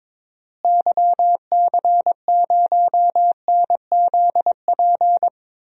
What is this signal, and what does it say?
Morse code, heard as a keyed tone, about 22 wpm, 700 Hz. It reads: YC0NZP
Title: DXCC-100 Call Signs - 6 Characters - Repeated 2x - 22wpm